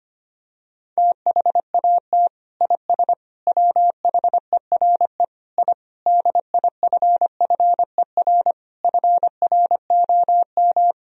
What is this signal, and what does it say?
Morse code, heard as a keyed tone, about 25 wpm, 700 Hz. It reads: THAT IS WHERE I DIFFER FROM
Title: Sentences from Top 1000 Words - Review Part 1 - 25wpm